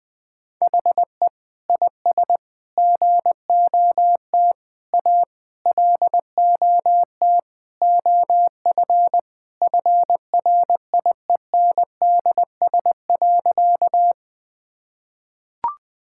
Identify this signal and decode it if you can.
Morse code, heard as a keyed tone, about 20 wpm, 700 Hz. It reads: HE IS GOT A LOT OF FRIENDS.
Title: Sentences with 4-Word Phrases - 20wpm